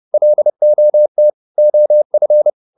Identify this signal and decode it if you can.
Morse code, heard as a keyed tone, about 30 wpm, 600 Hz. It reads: LOT OF